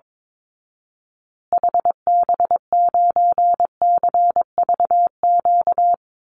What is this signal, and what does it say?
Morse code, heard as a keyed tone, about 22 wpm, 700 Hz. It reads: HB9CVQ
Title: Call Signs Repeated 3x - 22wpm